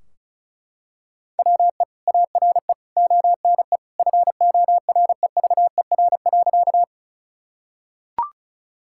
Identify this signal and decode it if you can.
Morse code, heard as a keyed tone, about 36 wpm, 700 Hz. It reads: WE ARE ONE FOREVER.
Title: Sentences from Top 1600 Words 35wpm